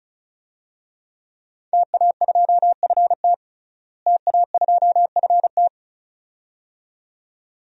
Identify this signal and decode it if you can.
Morse code, heard as a keyed tone, about 34 wpm, 700 Hz. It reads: TA2FT TA2FT
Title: DXCC-100 Call Signs - 5 Characters - Repeated 2x - 35wpm